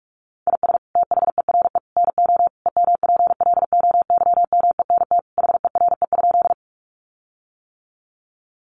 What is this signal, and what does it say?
Morse code, heard as a keyed tone, about 44 wpm, 700 Hz. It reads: IS THERE NO EMPLOYMENT HERE?
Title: Sentences from Top 1500 Words 45wpm